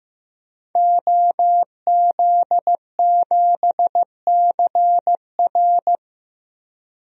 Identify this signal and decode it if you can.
Morse code, heard as a keyed tone, about 15 wpm, 700 Hz. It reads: OZ7CR
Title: DXCC-100 Call Signs - 5 Characters - 15wpm